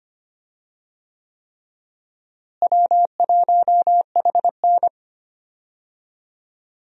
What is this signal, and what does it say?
Morse code, heard as a keyed tone, about 25 wpm, 700 Hz. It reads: W1HN
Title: USA Call Signs - 4 Characters - Repeated 3x - 25wpm